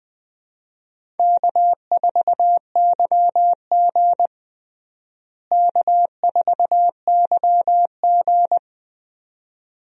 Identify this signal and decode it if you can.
Morse code, heard as a keyed tone, about 21 wpm, 700 Hz. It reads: K4YG K4YG
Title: USA Call Signs - 4 Characters - Repeated 2x - 20wpm